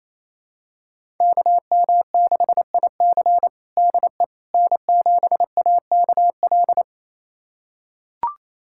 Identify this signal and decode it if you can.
Morse code, heard as a keyed tone, about 28 wpm, 700 Hz. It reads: KM6IC DE N7AKL
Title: USA Callsign DE Callsign - 5 Characters - 28wpm